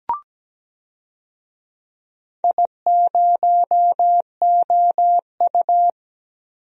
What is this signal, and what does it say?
Morse code, heard as a keyed tone, about 17 wpm, 700 Hz. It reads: I0OU
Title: DXCC-100 Call Signs - 4 Characters - Repeated 2x - 17wpm